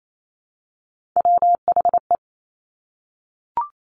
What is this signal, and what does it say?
Morse code, heard as a keyed tone, about 28 wpm, 700 Hz. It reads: WHE
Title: Top Three Letter Combinations in Words 28wpm